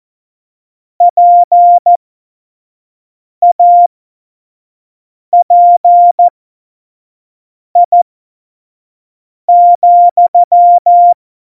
Morse code, heard as a keyed tone, about 14 wpm, spaced as for 5 wpm, 700 Hz. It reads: PAPI,
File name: Morsekurs / Witze / Witz 12. September 17, Tempo 25